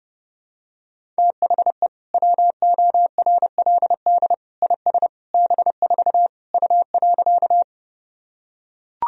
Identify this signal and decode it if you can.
Morse code, heard as a keyed tone, about 30 wpm, 700 Hz. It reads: THE WORLD IS B4 U.
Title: Sentences from Top 200 Words and Abbreviations 30wpm